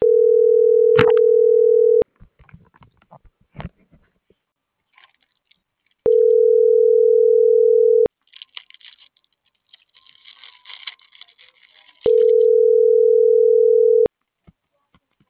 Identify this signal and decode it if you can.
Ringback tone (North America), 440 + 480 Hz.